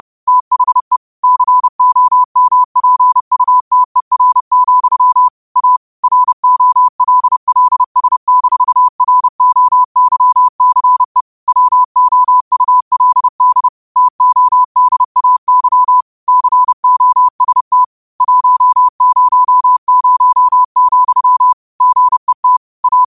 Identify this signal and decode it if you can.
Morse code, heard as a keyed tone, about 30 wpm, 1000 Hz. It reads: THE COMPUTER, A ROLLS-ROYCE WOULD TODAY COST 100, GET A
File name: qotdfiles / 30wpm